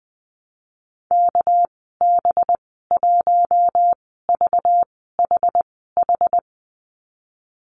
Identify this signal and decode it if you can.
Morse code, heard as a keyed tone, about 20 wpm, 700 Hz. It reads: K B 1 V H H
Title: 231 Lesson 37g - Letter x - Call Signs 20wpm